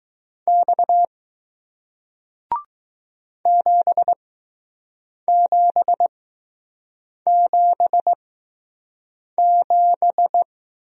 Morse code, heard as a keyed tone, about 21 wpm, spaced as for 6 wpm, 700 Hz. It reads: X 7777